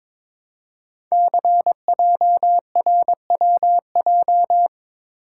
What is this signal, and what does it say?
Morse code, heard as a keyed tone, about 22 wpm, 700 Hz. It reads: CJRWJ
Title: Code Groups - 5 Letters - 22wpm